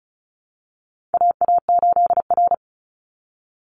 Morse code, heard as a keyed tone, about 34 wpm, 700 Hz. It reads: AA8R
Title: Call Signs Repeated 3x - 35wpm